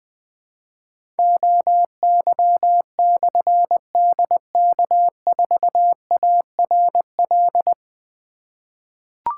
Morse code, heard as a keyed tone, about 20 wpm, 700 Hz. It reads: OY/DK4ARL
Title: Difficult Call Signs 20wpm